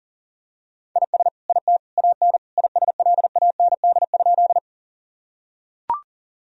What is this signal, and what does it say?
Morse code, heard as a keyed tone, about 37 wpm, 700 Hz. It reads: IS IT AN ISLAND?
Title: Sentences from Top 1000 Words - Review Part 3 - 40wpm